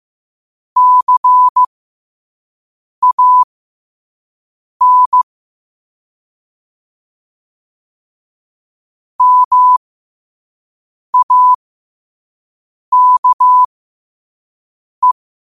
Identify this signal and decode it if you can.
Morse code, heard as a keyed tone, about 15 wpm, spaced as for 5 wpm, 1000 Hz. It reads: CAN MAKE
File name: qotdfiles / 5wpm